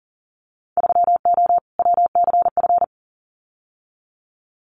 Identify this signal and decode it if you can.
Morse code, heard as a keyed tone, about 40 wpm, 700 Hz. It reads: 3O WCF
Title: Winter Field Day Exchanges Encore 40wpm